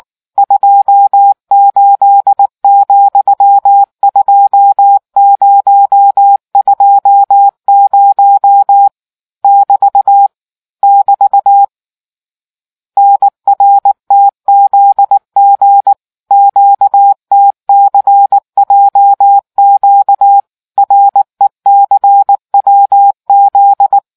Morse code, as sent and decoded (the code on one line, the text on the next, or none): ..--- ---.. --..-- ..--- ----- ..--- ----- / -...- / -...- / -. .-. - --.. --. / --.- - -.-. .--- --.- / .-. -.-. .-- --..
28,2020 <BT> <BT> NRTZG QTCJQ RECWZ